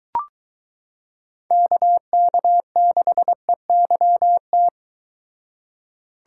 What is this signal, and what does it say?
Morse code, heard as a keyed tone, about 23 wpm, 700 Hz. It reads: KK6EYT